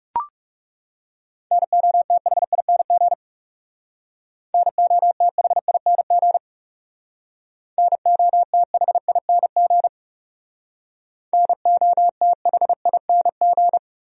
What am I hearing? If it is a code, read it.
Morse code, heard as a keyed tone, about 37 wpm, 700 Hz. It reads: NOTHING NOTHING NOTHING NOTHING
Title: Speed-Racing Top 500 Words 30wpm